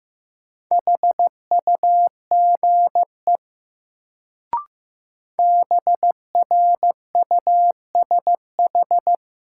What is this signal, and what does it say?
Morse code, heard as a keyed tone, about 15 wpm, 700 Hz. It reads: HUGE BRUSH
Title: Sentences from Top 1100 Words - New Word Drill - 15wpm